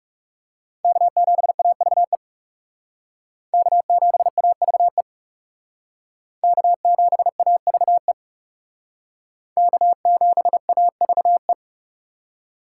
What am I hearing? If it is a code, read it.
Morse code, heard as a keyed tone, about 38 wpm, 700 Hz. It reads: K7AVE K7AVE K7AVE K7AVE